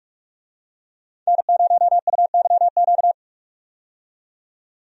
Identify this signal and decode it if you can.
Morse code, heard as a keyed tone, about 46 wpm, 700 Hz. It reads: N0UYQ
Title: USA Call Signs - 5 Characters - 45wpm